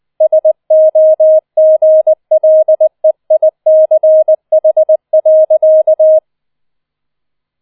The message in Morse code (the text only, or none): SOGLEICH.